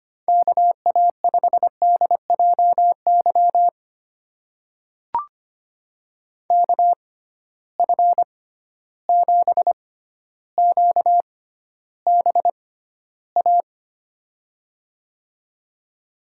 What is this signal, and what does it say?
Morse code, heard as a keyed tone, about 25 wpm, spaced as for 13 wpm, 700 Hz. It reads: KA5DJY KF7QBA
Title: USA Call Signs - 6x Character Spacing - 25wpm